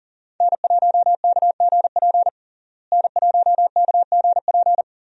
Morse code, heard as a keyed tone, about 40 wpm, 700 Hz. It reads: N1KGP N1KGP
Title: USA Call Signs - 5 Characters - Repeated 3x - 40wpm